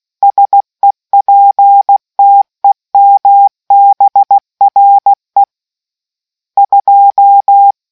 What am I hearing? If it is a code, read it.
Morse code, heard as a keyed tone, about 16 wpm, 800 Hz. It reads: SEPTEMBRE 2